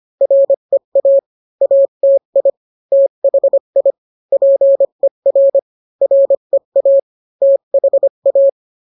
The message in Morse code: REA ATI THI PER REA THA